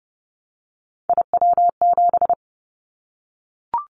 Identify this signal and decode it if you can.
Morse code, heard as a keyed tone, about 30 wpm, 700 Hz. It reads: IW7
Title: Callsign Prefixes for Top 25 DXCC - 30wpm